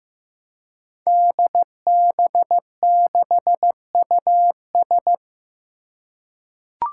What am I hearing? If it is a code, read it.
Morse code, heard as a keyed tone, about 15 wpm, 700 Hz. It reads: DB6US
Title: Five Challenging Letter-Number Combinations 15wpm